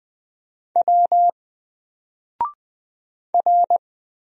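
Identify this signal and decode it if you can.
Morse code, heard as a keyed tone, about 20 wpm, 700 Hz. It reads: W R
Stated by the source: Single letter and number Morse drill at 20 wpm, from N4LCD.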